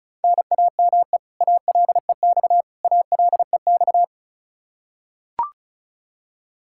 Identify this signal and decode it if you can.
Morse code, heard as a keyed tone, about 36 wpm, 700 Hz. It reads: NAME ALEX ALEX
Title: Practice QSO - Exchange 1 - Beginning - 35wpm